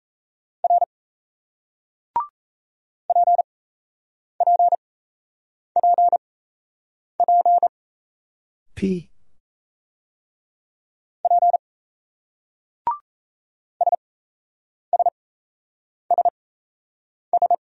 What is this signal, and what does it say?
Morse code, heard as a keyed tone, about 38 wpm, spaced as for 8 wpm, 700 Hz. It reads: R PPPP P SSSS